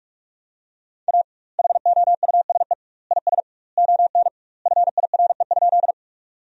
Morse code, heard as a keyed tone, about 43 wpm, 700 Hz. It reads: A HOUSE IS ON FIRE?
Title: 2x Word-Spacing - Sentences from Top 400 Words - 45wpm